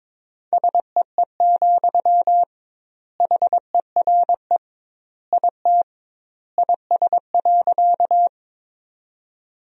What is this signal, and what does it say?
Morse code, heard as a keyed tone, about 22 wpm, 700 Hz. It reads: SEE, HERE IT IS.
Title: 2x Word-Spacing - Sentences from Top 200 Words - 22wpm